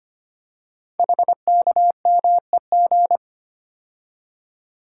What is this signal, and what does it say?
Morse code, heard as a keyed tone, about 25 wpm, 690 Hz. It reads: HKMEG
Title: Code Groups - 5 Letters - 25wpm